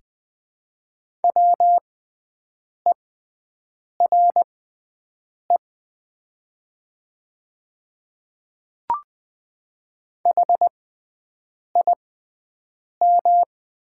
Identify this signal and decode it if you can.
Morse code, heard as a keyed tone, about 20 wpm, spaced as for 7 wpm, 700 Hz. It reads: WERE HIM
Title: 6x Character Spacing - Top 100 Words - 20wpm